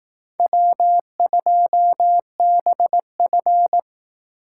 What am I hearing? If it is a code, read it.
Morse code, heard as a keyed tone, about 18 wpm, 700 Hz. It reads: W2BF
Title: Speed Racing - USA 1x2 Call Signs 15wpm